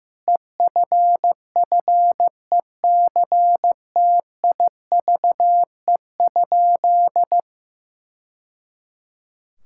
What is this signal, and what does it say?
Morse code, heard as a keyed tone, about 15 wpm, 700 Hz. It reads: EFFECTIVE?
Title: Sentences from Top 1000 Words - Review Part 6 - 15wpm